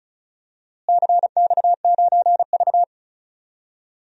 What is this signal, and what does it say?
Morse code, heard as a keyed tone, about 36 wpm, 700 Hz. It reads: CX9V
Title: DXCC-100 Call Signs - 4 Characters - Repeated 2x - 35wpm